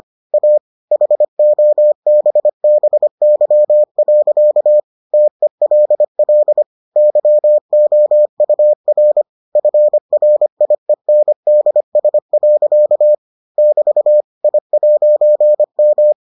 Morse code, heard as a keyed tone, about 25 wpm, 600 Hz. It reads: A HOBBY. TELL YOUR FRIENDS. <BT> I'M